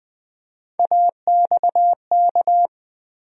Morse code, heard as a keyed tone, about 21 wpm, 700 Hz. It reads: AXK
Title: USA Call Sign Postfixes - 3 Letters 20wpm